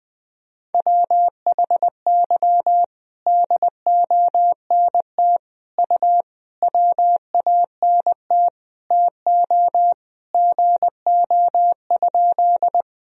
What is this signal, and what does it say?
Morse code, heard as a keyed tone, about 21 wpm, 700 Hz. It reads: WHY DONT U WANT TO GO?
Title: Sentences from Top 300 Words and Abbreviations 20wpm